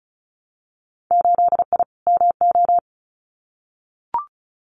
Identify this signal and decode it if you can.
Morse code, heard as a keyed tone, about 34 wpm, 700 Hz. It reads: 8I MO
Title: Winter Field Day Exchanges Encore 35wpm